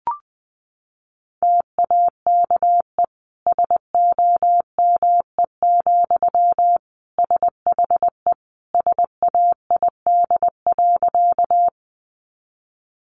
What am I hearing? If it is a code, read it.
Morse code, heard as a keyed tone, about 20 wpm, 700 Hz. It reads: TAKE SOME, SHE SAID.
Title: Sentences from Top 200 Words 20wpm